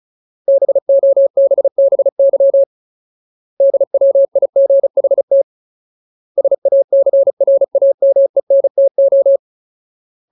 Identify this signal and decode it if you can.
Morse code, heard as a keyed tone, about 34 wpm, 550 Hz. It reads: BOBBY DWIGHT SACRAMENTO
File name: Ultimate Word List / 35WPM Ultimate